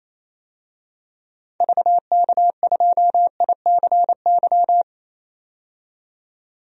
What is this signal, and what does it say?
Morse code, heard as a keyed tone, about 28 wpm, 700 Hz. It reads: VK2ICY
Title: DXCC-100 Call Signs - 6 Characters - 28wpm